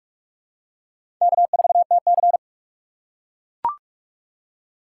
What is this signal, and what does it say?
Morse code, heard as a keyed tone, about 42 wpm, 700 Hz. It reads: K4TC
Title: USA Call Signs - 4 Characters - 2x Character Spacing - 45wpm